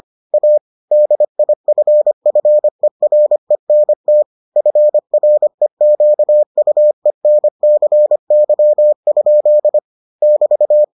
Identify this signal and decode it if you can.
Morse code, heard as a keyed tone, about 25 wpm, 600 Hz. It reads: A DIFFERENT FREQUENCY? <BT>